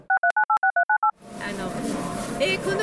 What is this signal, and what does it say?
DTMF, heard as telephone touch tones, tones of about 100 ms, with speech. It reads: 63986398